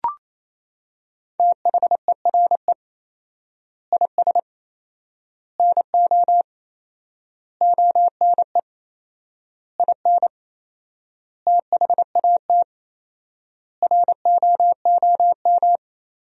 Morse code, heard as a keyed tone, about 28 wpm, 700 Hz. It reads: THERE IS NO ONE IN THAT ROOM
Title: 4x Word-Spacing - Sentences from Top 500 Words - 28wpm